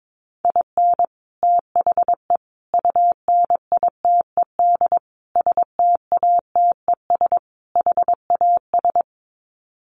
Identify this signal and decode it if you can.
Morse code, heard as a keyed tone, about 22 wpm, 700 Hz. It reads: IN THE UNITED STATES HAS
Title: Sets of 5 Words Encore2 22wpm